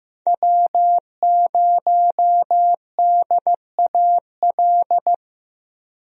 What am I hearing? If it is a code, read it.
Morse code, heard as a keyed tone, about 15 wpm, 700 Hz. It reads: W0DAL
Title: USA Call Signs - 5 Characters - Repeated 3x - 15wpm